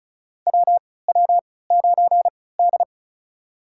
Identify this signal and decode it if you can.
Morse code, heard as a keyed tone, about 36 wpm, spaced as for 20 wpm, 700 Hz. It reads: WW9D
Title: USA Call Signs - 4 Characters - 3x Character Spacing - 35wpm